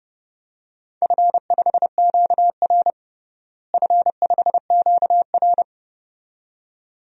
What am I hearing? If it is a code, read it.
Morse code, heard as a keyed tone, about 30 wpm, 700 Hz. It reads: F5QR F5QR
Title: DXCC-100 Call Signs - 4 Characters - Repeated 2x - 30wpm